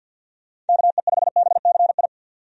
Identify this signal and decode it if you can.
Morse code, heard as a keyed tone, about 50 wpm, 700 Hz. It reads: KE5BCI
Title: USA Call Signs - 6 Characters - 50wpm